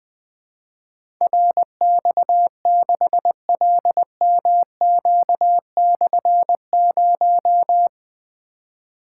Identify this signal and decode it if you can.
Morse code, heard as a keyed tone, about 20 wpm, 700 Hz. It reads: RX6LMQ/0
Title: Difficult Call Signs 20wpm